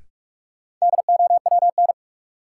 Morse code, heard as a keyed tone, about 46 wpm, 700 Hz. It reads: DOWN
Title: Sentences from Top 300 Words - New Word Drill - 45wpm